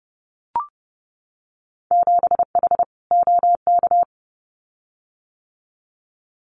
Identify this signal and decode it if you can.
Morse code, heard as a keyed tone, about 30 wpm, 700 Hz. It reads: 7H OK
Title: Winter Field Day Exchanges Encore 30wpm